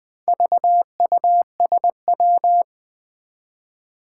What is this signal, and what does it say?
Morse code, heard as a keyed tone, about 20 wpm, 700 Hz. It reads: VUSW
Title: Four Challenging Letter-Number Combinations 20wpm